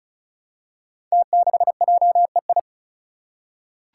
Morse code, heard as a keyed tone, about 36 wpm, 700 Hz. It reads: T6JEI
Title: Five Challenging Letter-Number Combinations 35wpm